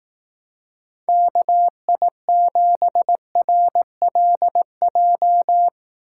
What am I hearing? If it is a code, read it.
Morse code, heard as a keyed tone, about 18 wpm, 700 Hz. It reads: KI7RLJ
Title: Speed Racing - USA 2x3 Call Signs 15wpm